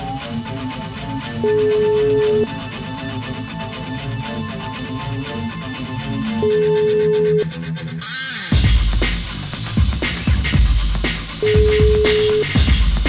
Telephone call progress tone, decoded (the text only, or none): ringback tone